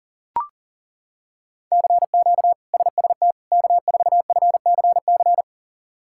Morse code, heard as a keyed tone, about 40 wpm, 700 Hz. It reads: CQ SST K4FCC
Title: K1USN SST QSO Practice - 40wpm